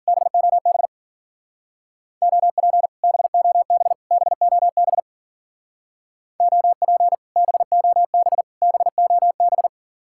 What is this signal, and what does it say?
Morse code, heard as a keyed tone, about 52 wpm, 690 Hz. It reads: BOB OP BOB BOB OP BOB BOB